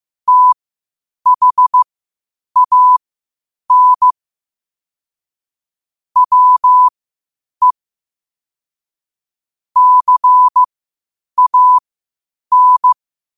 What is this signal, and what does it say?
Morse code, heard as a keyed tone, about 15 wpm, spaced as for 8 wpm, 1000 Hz. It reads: THAN WE CAN
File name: qotdfiles / 8wpm